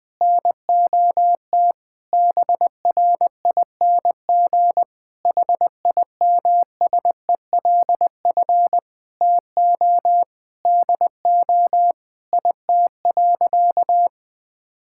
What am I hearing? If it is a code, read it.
Morse code, heard as a keyed tone, about 21 wpm, 700 Hz. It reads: NOT BRING HIMSELF TO DO IT.
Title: Sentences from Top 1600 Words 20wpm